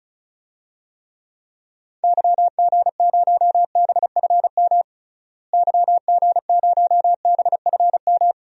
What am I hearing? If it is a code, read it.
Morse code, heard as a keyed tone, about 36 wpm, 700 Hz. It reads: YG0BFM YG0BFM
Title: DXCC-100 Call Signs - 6 Characters - Repeated 2x - 35wpm